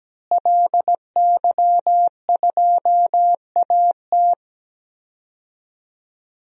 Morse code, heard as a keyed tone, about 17 wpm, 700 Hz. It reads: LY2AT
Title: DXCC-100 Call Signs - 5 Characters - Repeated 2x - 17wpm